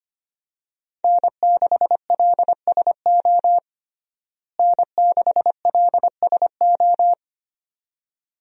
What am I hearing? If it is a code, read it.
Morse code, heard as a keyed tone, about 25 wpm, 700 Hz. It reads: N6LSO N6LSO
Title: USA Call Signs - 5 Characters - Repeated 2x - 25wpm